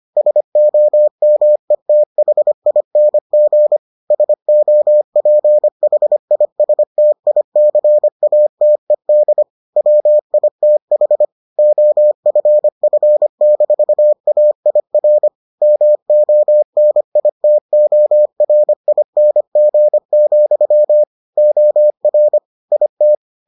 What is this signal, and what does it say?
Morse code, heard as a keyed tone, about 25 wpm, 600 Hz. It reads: SOMETHING SOPHISTICATED WITH OFF-AIR MONITORING, OR IT